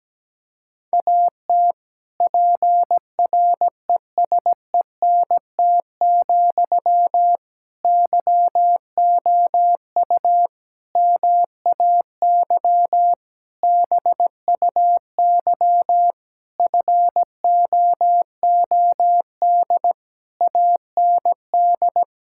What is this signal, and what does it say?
Morse code, heard as a keyed tone, about 17 wpm, 700 Hz. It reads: AT PRESENT, YOU MAY BUY FOOD AND
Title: Sentences from Top 1000 Words - Review Part 5 - 17wpm